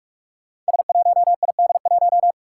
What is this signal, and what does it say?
Morse code, heard as a keyed tone, about 44 wpm, 700 Hz. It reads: S1ID1
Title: Five Challenging Letter-Number Combinations 45wpm